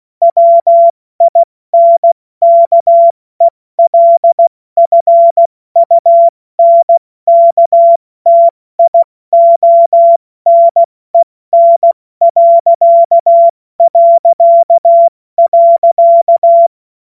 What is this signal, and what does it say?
Morse code, heard as a keyed tone, about 16 wpm, 680 Hz. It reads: WINKELFUNKTIONEN...
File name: Morsekurs / Witze / Witz 16. Dezember 2025 70 F80